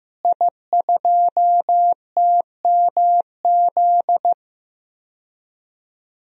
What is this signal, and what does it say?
Morse code, heard as a keyed tone, about 15 wpm, 700 Hz. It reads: I2TMZ
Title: DXCC-100 Call Signs - 5 Characters - Repeated 3x - 15wpm